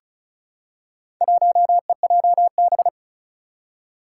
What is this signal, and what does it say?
Morse code, heard as a keyed tone, about 36 wpm, 700 Hz. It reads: 1EJB
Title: Four Challenging Letter-Number Combinations 35wpm